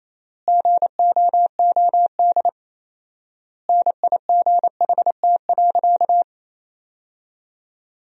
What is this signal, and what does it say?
Morse code, heard as a keyed tone, about 27 wpm, 700 Hz. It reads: GOOD NIGHT.
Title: 4x Word-Spacing - Sentences from Top 300 Words - 28wpm